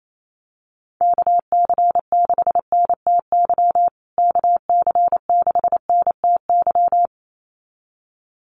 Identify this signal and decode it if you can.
Morse code, heard as a keyed tone, about 28 wpm, 700 Hz. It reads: KC6NTY KC6NTY
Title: USA Call Signs Repeated 2x 28wpm